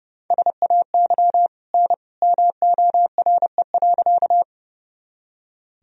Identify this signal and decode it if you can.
Morse code, heard as a keyed tone, about 32 wpm, 700 Hz. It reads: SAY N MORE.
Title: Sentences from Top 200 Words and Abbreviations 30wpm